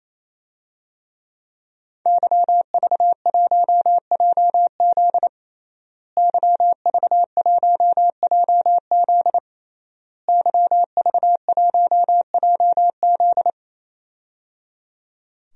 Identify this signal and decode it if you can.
Morse code, heard as a keyed tone, about 28 wpm, 700 Hz. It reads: YV1JZ YV1JZ YV1JZ
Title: DXCC-100 Call Signs - 5 Characters - Repeated 3x - 28wpm